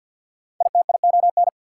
Morse code, heard as a keyed tone, about 52 wpm, 690 Hz. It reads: ITION
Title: Prefixes and Suffixes 50wpm